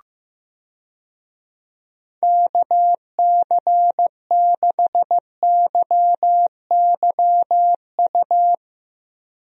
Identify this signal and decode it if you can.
Morse code, heard as a keyed tone, about 15 wpm, 700 Hz. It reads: KC6YYU
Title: USA Call Signs - 6 Characters - Repeated 2x - 15wpm